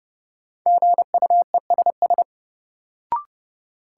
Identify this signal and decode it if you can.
Morse code, heard as a keyed tone, about 30 wpm, 700 Hz. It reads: GUESS
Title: Sentences from Top 1500 Words - New Word Drill - 30wpm